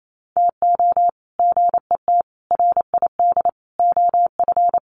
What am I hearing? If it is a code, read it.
Morse code, heard as a keyed tone, about 28 wpm, 700 Hz. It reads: TO GET RID OF
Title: Sets of 4 Words 28wpm